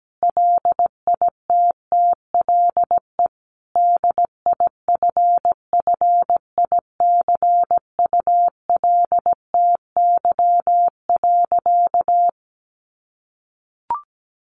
Morse code, heard as a keyed tone, about 17 wpm, 700 Hz. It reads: LITTLE DIFFICULTY.
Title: Sentences from Top 1500 Words 17wpm